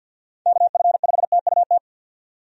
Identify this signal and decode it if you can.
Morse code, heard as a keyed tone, about 49 wpm, 700 Hz. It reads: KF5TUT
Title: USA Call Signs - 6 Characters - 50wpm